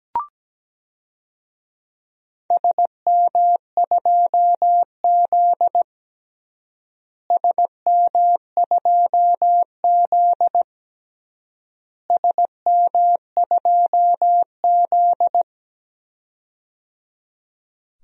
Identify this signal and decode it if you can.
Morse code, heard as a keyed tone, about 17 wpm, 700 Hz. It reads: SM2Z SM2Z SM2Z
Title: DXCC-100 Call Signs - 4 Characters - Repeated 3x - 17wpm